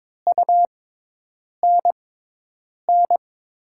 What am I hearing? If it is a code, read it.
Morse code, heard as a keyed tone, about 22 wpm, spaced as for 8 wpm, 700 Hz. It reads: UNN
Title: USA Call Signs - 5 Characters - 6x Character Spacing - 22wpm